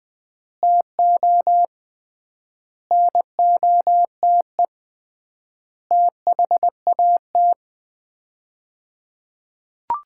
Morse code, heard as a keyed tone, about 20 wpm, 700 Hz. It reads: TO NOTE THAT
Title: 3x Word-Spacing - Sets of 5 Words - 20wpm